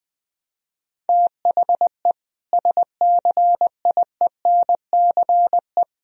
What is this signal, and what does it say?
Morse code, heard as a keyed tone, about 20 wpm, 700 Hz. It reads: THE SCIENCE
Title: Sentences from Top 1000 Words - Review Part 2 - 20wpm